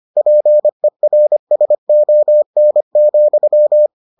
Morse code, heard as a keyed tone, about 25 wpm, 600 Hz. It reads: PERSON,